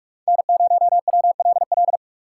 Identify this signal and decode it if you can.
Morse code, heard as a keyed tone, about 46 wpm, 700 Hz. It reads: N0WLL
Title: USA Call Signs - 5 Characters - 6x Character Spacing - 45wpm